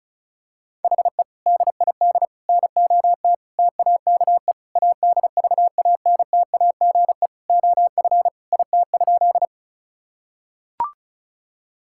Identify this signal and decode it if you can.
Morse code, heard as a keyed tone, about 34 wpm, 700 Hz. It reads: HE DID NOT TAKE ADVANTAGE OF IT?
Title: Sentences from Top 1000 Words - Review Part 6 - 35wpm